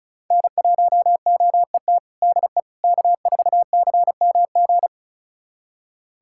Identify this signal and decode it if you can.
Morse code, heard as a keyed tone, about 34 wpm, 700 Hz. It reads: N1OET DE K4CMG
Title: USA Callsign DE Callsign - 5 Characters - 35wpm